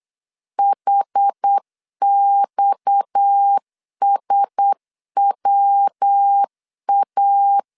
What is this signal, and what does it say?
Morse code, heard as a keyed tone, about 8 wpm, 800 Hz. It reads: HXSWA
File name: CW / DET / 1 6 1500C S B 8wpm 5142